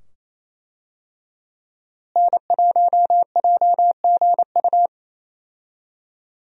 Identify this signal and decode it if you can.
Morse code, heard as a keyed tone, about 28 wpm, 700 Hz. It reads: N1JGU
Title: USA Call Signs - 5 Characters - Repeated 2x - 28wpm